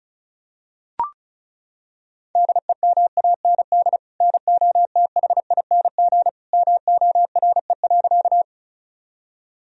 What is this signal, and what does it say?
Morse code, heard as a keyed tone, about 35 wpm, 700 Hz. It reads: DEMAND NOTHING MORE.
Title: Sentences from Top 1000 Words - Review Part 4 - 35wpm